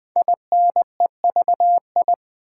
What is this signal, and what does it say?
Morse code, heard as a keyed tone, about 20 wpm, 700 Hz. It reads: INEVI
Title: Build A Word - 5 Letters At A Time - 10-Letter Words - 20wpm